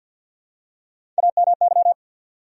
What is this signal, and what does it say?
Morse code, heard as a keyed tone, about 50 wpm, 700 Hz. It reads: AMY
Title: USA Common Names - Max 5 Letters - 50wpm